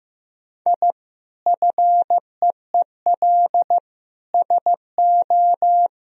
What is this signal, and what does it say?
Morse code, heard as a keyed tone, about 15 wpm, 700 Hz. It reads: I FEEL SO
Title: Sentences from Top 1000 Words - Review Part 3 - 15wpm